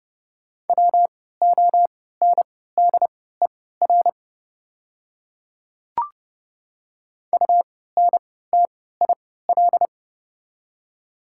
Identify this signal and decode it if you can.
Morse code, heard as a keyed tone, about 30 wpm, spaced as for 17 wpm, 690 Hz. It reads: WONDER UNTIL
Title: 3x Character Spacing - Top 500 Words - 30wpm